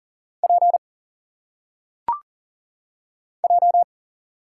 Morse code, heard as a keyed tone, about 40 wpm, 700 Hz. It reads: P J
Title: Single Letters - Phonetic Pronunciation 40wpm